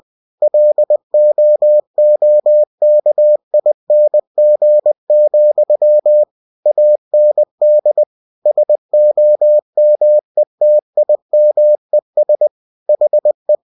Morse code, heard as a keyed tone, about 20 wpm, 600 Hz. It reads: LOOKING, AND SOMETIMES HE
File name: cw-books / winnie-the-pooh / winnie-20wpm-page-0092